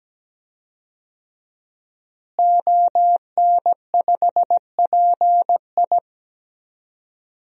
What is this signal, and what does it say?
Morse code, heard as a keyed tone, about 17 wpm, 700 Hz. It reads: ON5PI
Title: DXCC-100 Call Signs - 5 Characters - Repeated 3x - 17wpm